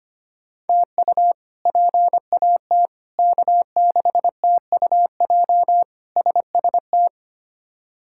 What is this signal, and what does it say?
Morse code, heard as a keyed tone, about 25 wpm, 700 Hz. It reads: TU PAT K6TUJ SST
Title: K1USN SST QSO Practice - 25wpm